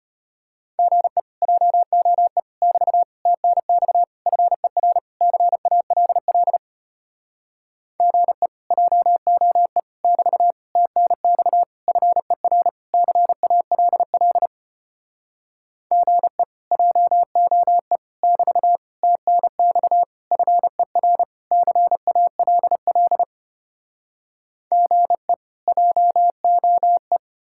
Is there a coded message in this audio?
Yes: Morse code from 0.8 to 27.2 s, heard as a keyed tone, about 33 wpm, 700 Hz. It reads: GE JOE <BT> TNX FER CALL GE JOE <BT> TNX FER CALL GE JOE <BT> TNX FER CALL GE JOE